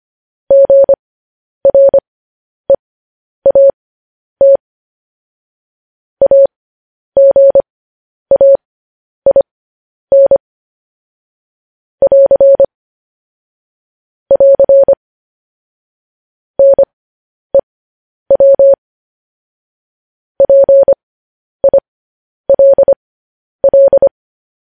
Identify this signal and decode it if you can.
Morse code, heard as a keyed tone, about 25 wpm, spaced as for 10 wpm, 550 Hz. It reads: GREAT AGAIN <AR> <AR> NEW PILL
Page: news headlines in morse code at 10 wpm